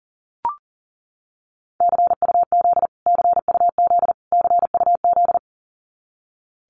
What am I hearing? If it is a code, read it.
Morse code, heard as a keyed tone, about 40 wpm, 700 Hz. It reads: CUZ CUZ CUZ
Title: Abbreviations Repeated 3x - 40wpm